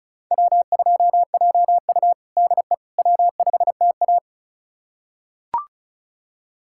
Morse code, heard as a keyed tone, about 36 wpm, 700 Hz. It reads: W2JU DE W5TA
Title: USA Callsign DE Callsign - 4 Characters - 35wpm